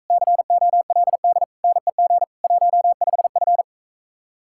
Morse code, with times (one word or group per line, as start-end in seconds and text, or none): 0.1-1.4 COLD
1.6-2.2 NEG
2.4-3.6 15F